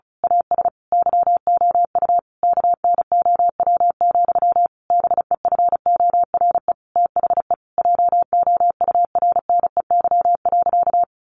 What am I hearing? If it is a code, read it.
Morse code, heard as a keyed tone, about 34 wpm, 700 Hz. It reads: AS YOU KNOW, BEFORE THE JOURNEY.
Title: A Sound Of Thunder 35wpm